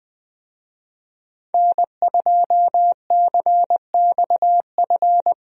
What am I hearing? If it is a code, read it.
Morse code, heard as a keyed tone, about 20 wpm, 700 Hz. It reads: N2CXF
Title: USA Call Signs - 5 Characters - Repeated 3x - 20wpm